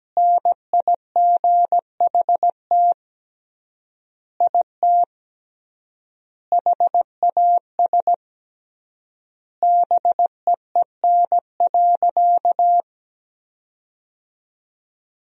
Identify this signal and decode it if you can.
Morse code, heard as a keyed tone, about 17 wpm, 700 Hz. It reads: NIGHT IT HAS BEEN.
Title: 3x Word-Spacing - Sentences from Top 300 Words - 17wpm